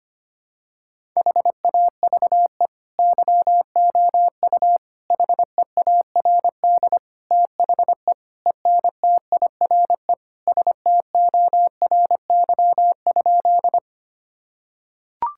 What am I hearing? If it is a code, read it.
Morse code, heard as a keyed tone, about 25 wpm, 700 Hz. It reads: HAVE YOU HEARD THE ENTIRE STORY?
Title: Sentences from Top 1000 Words - Review Part 5 - 25wpm